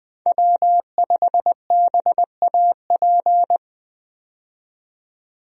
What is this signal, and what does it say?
Morse code, heard as a keyed tone, about 21 wpm, 700 Hz. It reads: W5BAP
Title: USA Call Signs - 5 Characters - 20wpm